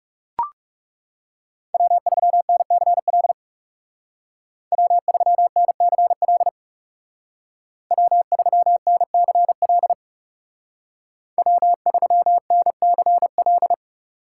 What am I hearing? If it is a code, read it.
Morse code, heard as a keyed tone, about 39 wpm, 700 Hz. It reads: W3NCL W3NCL W3NCL W3NCL